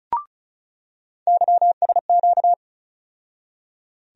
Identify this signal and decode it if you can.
Morse code, heard as a keyed tone, about 36 wpm, 700 Hz. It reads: YSQ
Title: USA Call Sign Postfixes - 3 Letters 35wpm